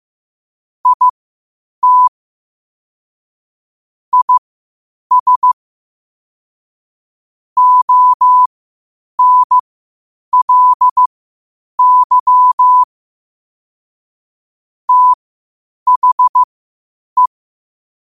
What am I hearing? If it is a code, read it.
Morse code, heard as a keyed tone, about 15 wpm, spaced as for 8 wpm, 1000 Hz. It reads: IT IS ONLY THE
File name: qotdfiles / 8wpm